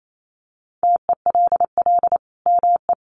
Morse code, heard as a keyed tone, about 28 wpm, 700 Hz. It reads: TELL ME
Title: Sentences from Top 800 Words 28wpm